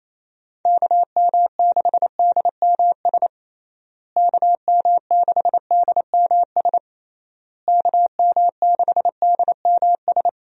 Morse code, heard as a keyed tone, about 28 wpm, 700 Hz. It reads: KM6DMS KM6DMS KM6DMS
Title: USA Call Signs - 6 Characters - Repeated 3x - 28wpm